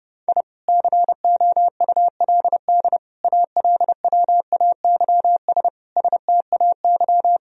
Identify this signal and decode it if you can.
Morse code, heard as a keyed tone, about 32 wpm, 700 Hz. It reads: I COULD ALWAYS STAY
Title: Sentences from Top 1000 Words - Review Part 1 - 30wpm